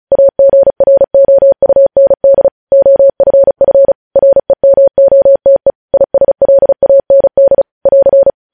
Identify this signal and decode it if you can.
Morse code, heard as a keyed tone, about 36 wpm, 550 Hz. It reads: AGROUND OFF REMOTE ISLAND <AR>
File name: podcast / News headlines-1x-35x35